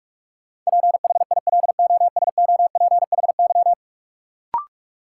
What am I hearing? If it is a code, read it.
Morse code, heard as a keyed tone, about 46 wpm, 700 Hz. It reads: PHILOSOPHY
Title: Sentences from Top 900 Words - New Word Drill - 45wpm